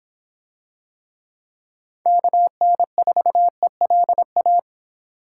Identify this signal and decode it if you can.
Morse code, heard as a keyed tone, about 26 wpm, 700 Hz. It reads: KN4ELA